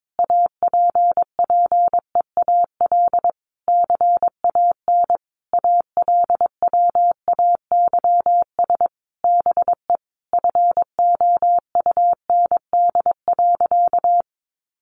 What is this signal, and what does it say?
Morse code, heard as a keyed tone, about 22 wpm, 700 Hz. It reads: APPEAL CAN ALWAYS BE FOUND.
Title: Sentences from Top 1300 Words 22wpm